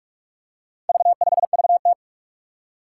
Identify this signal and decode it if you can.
Morse code, heard as a keyed tone, about 44 wpm, 700 Hz. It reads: V5VT
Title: Four Challenging Letter-Number Combinations 45wpm